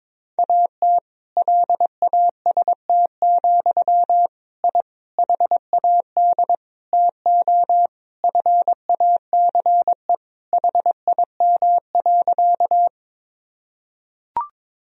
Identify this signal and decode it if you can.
Morse code, heard as a keyed tone, about 22 wpm, 700 Hz. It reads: AT LAST, I HAD TO FACE HIM.
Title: Sentences from Top 1000 Words - Review Part 2 - 22wpm